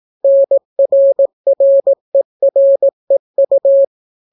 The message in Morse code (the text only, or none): NRREREU